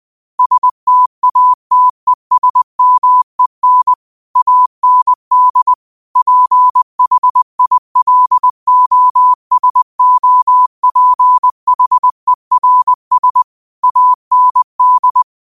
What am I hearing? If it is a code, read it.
Morse code, heard as a keyed tone, about 20 wpm, 1000 Hz. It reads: STATESMEN AND PHILOSOPHERS AND
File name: qotdfiles / 20wpm